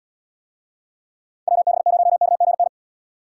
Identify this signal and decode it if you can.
Morse code, heard as a keyed tone, about 73 wpm, 700 Hz. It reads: WB0KGN